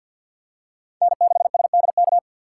Morse code, heard as a keyed tone, about 52 wpm, 700 Hz. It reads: N6SDK
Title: USA Call Signs - 5 Characters - 6x Character Spacing - 50wpm